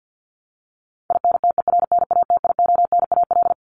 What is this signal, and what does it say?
Morse code, heard as a keyed tone, about 50 wpm, 700 Hz. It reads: INTERNATIONAL
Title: Top 501-750 Words 50wpm